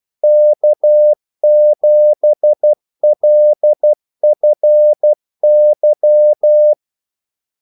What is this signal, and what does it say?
Morse code, heard as a keyed tone, about 12 wpm, 600 Hz. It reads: K7LFY
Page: 2 ekstra ordmellemrum mellem kaldesignalerne